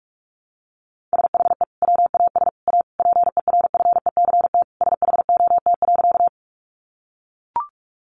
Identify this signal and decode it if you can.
Morse code, heard as a keyed tone, about 44 wpm, 700 Hz. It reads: SHE WAS A PERFECT SHOT.
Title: Sentences from Top 1100 Words 45wpm